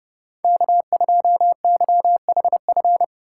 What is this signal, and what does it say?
Morse code, heard as a keyed tone, about 31 wpm, 700 Hz. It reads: K2YHF
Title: USA Call Signs - 5 Characters - 8x Character Spacing - 30wpm